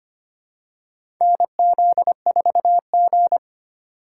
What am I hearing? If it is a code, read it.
Morse code, heard as a keyed tone, about 25 wpm, 700 Hz. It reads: NZ4G
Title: USA Call Signs - 4 Characters - 4x Character Spacing - 25wpm